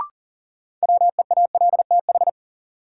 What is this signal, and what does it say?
Morse code, heard as a keyed tone, about 40 wpm, 700 Hz. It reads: WEALTH
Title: Sentences from Top 1000 Words - Review Part 4 - New Word Drill - 40wpm